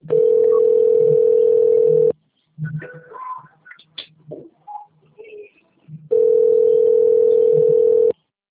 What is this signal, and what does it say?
Ringback tone (North America), 440 + 480 Hz.